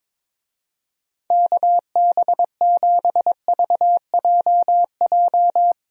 Morse code, heard as a keyed tone, about 22 wpm, 700 Hz. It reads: KB7VJJ
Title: USA Call Signs - 6 Characters - Repeated 3x - 22wpm